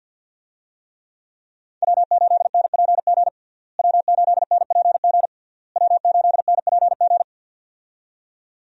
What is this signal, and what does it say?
Morse code, heard as a keyed tone, about 46 wpm, 700 Hz. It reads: W8NPG W8NPG W8NPG
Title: USA Call Signs - 5 Characters - Repeated 3x - 50wpm